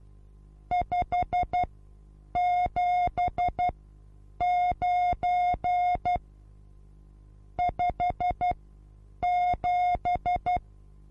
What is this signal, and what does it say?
Morse code, heard as a keyed tone, about 12 wpm, spaced as for 8 wpm, 720 Hz. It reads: 579 57
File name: repos / Morsekurs / DJ4UF Morsen23